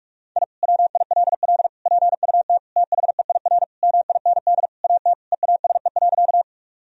Morse code, heard as a keyed tone, about 41 wpm, 700 Hz. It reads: I WILL PUT THEIR MIND AT EASE.
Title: Sentences from Top 1000 Words - Review Part 1 - 45wpm